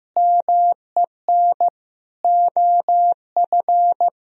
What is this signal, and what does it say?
Morse code, heard as a keyed tone, about 15 wpm, 700 Hz. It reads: MEN OF
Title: Sentences from Top 1000 Words - Review Part 2 - 15wpm